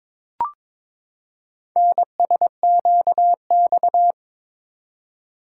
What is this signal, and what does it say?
Morse code, heard as a keyed tone, about 22 wpm, 700 Hz. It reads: NSQX
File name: practice / 22wpm / Four Letter-Number Combinations - 22wpm